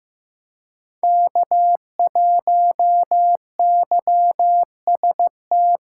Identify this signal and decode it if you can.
Morse code, heard as a keyed tone, about 18 wpm, 700 Hz. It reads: K1YST
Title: Speed Racing - USA Call Signs 15wpm